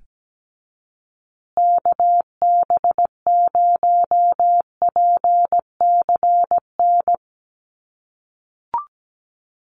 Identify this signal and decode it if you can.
Morse code, heard as a keyed tone, about 17 wpm, 700 Hz. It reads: KB0PCN
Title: USA Call Signs Repeated 2x 17wpm